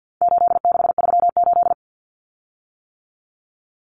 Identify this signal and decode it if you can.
Morse code, heard as a keyed tone, about 50 wpm, 700 Hz. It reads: 8638
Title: Common Cut Numbers - 4-Digit 50wpm